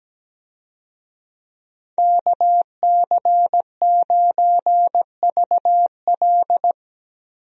Morse code, heard as a keyed tone, about 17 wpm, 700 Hz. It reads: KC9VL
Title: USA Call Signs - 5 Characters - 17wpm